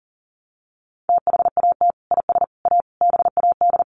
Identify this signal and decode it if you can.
Morse code, heard as a keyed tone, about 40 wpm, 700 Hz. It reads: THAT IS A BAD
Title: Sentences from Top 700 Words 40wpm